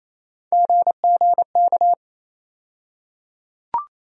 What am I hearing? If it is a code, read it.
Morse code, heard as a keyed tone, about 28 wpm, 700 Hz. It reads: GGK
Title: USA Call Sign Postfixes - 3 Letters 28wpm